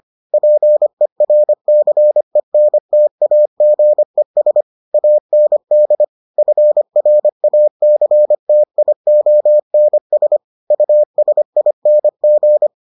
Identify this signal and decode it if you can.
Morse code, heard as a keyed tone, about 25 wpm, 600 Hz. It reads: PERCENTAGES AND FRACTIONS USING